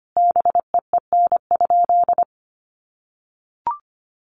Morse code, heard as a keyed tone, about 25 wpm, 700 Hz. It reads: BEEN?
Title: Sentences from Top 800 Words 25wpm